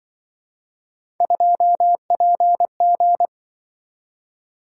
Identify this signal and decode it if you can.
Morse code, heard as a keyed tone, about 23 wpm, 700 Hz. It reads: 2PG
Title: Speed-Racing - Three Letter-Number Combinations - 20wpm